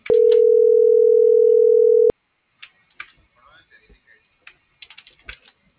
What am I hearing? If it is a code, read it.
Ringback tone (North America), 440 + 480 Hz.